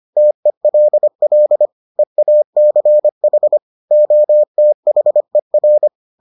Morse code, heard as a keyed tone, about 25 wpm, 600 Hz. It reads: TELL EACH OTHER